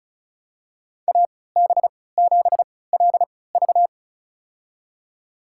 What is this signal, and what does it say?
Morse code, heard as a keyed tone, about 36 wpm, spaced as for 20 wpm, 700 Hz. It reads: AB7LV
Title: USA Call Signs - 5 Characters - 3x Character Spacing - 35wpm